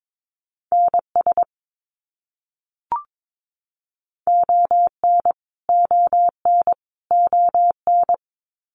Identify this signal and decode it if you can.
Morse code, heard as a keyed tone, about 22 wpm, 700 Hz. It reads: NS ON ON ON
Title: Canadian Provinces and Territories - Repeated 3x - 22wpm